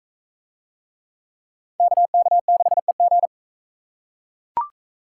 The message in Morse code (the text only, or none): KK6EG